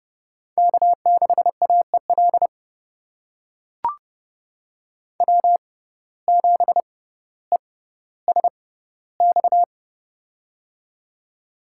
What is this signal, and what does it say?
Morse code, heard as a keyed tone, about 30 wpm, spaced as for 15 wpm, 700 Hz. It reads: K6AEL W7ESX
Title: USA Call Signs - 5 Characters - 6x Character Spacing - 30wpm